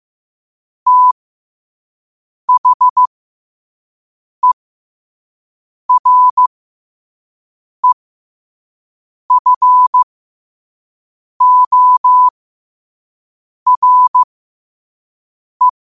Morse code, heard as a keyed tone, about 15 wpm, spaced as for 5 wpm, 1000 Hz. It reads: THEREFORE
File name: qotdfiles / 5wpm